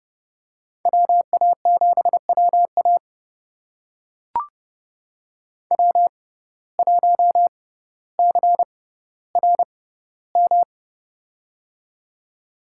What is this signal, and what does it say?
Morse code, heard as a keyed tone, about 30 wpm, spaced as for 15 wpm, 700 Hz. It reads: WA7WA W1CRM
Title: USA Call Signs - 5 Characters - 6x Character Spacing - 30wpm